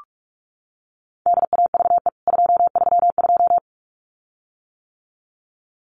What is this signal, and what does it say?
Morse code, heard as a keyed tone, about 44 wpm, 700 Hz. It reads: DAVE 232
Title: CWT Exchange 45wpm